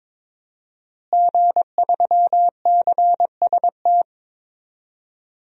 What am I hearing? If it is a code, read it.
Morse code, heard as a keyed tone, about 22 wpm, 700 Hz. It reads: G3CST
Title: DXCC-100 Call Signs - 5 Characters - Repeated 3x - 22wpm